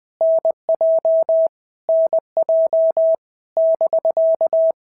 Morse code, heard as a keyed tone, about 20 wpm, 650 Hz. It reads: NJ NJ <BK>
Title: POTA QSO Practice - 20wpm